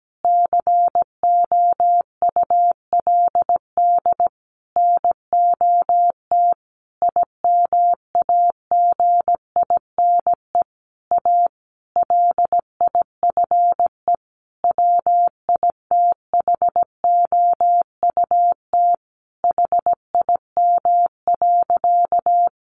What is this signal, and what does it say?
Morse code, heard as a keyed tone, about 17 wpm, 700 Hz. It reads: COULD NOT IMAGINE A LIFE WITHOUT HIM.
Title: Sentences from Top 1400 Words 17wpm